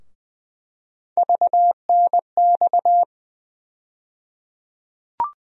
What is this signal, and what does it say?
Morse code, heard as a keyed tone, about 20 wpm, 700 Hz. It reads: VNX
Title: USA Call Sign Postfixes - 3 Letters 20wpm